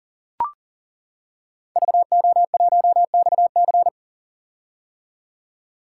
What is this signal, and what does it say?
Morse code, heard as a keyed tone, about 40 wpm, 700 Hz. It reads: VO1XC
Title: K1USN SST QSO Practice - 40wpm